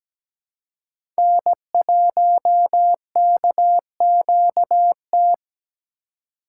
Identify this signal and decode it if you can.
Morse code, heard as a keyed tone, about 17 wpm, 700 Hz. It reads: N1KQT
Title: USA Call Signs - 5 Characters - 2x Character Spacing - 17wpm